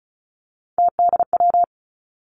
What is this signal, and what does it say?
Morse code, heard as a keyed tone, about 34 wpm, 700 Hz. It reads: TDW
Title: Three Challenging Letter-Number Combinations 35wpm